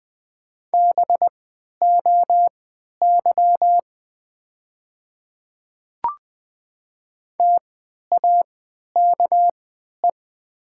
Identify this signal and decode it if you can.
Morse code, heard as a keyed tone, about 20 wpm, spaced as for 11 wpm, 700 Hz. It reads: BOY TAKE
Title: 3x Character Spacing - Top 200 Words - 20wpm